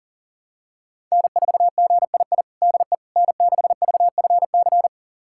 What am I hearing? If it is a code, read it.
Morse code, heard as a keyed tone, about 37 wpm, 700 Hz. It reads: N4GII DE N6VFC
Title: USA Callsign DE Callsign - 5 Characters - 40wpm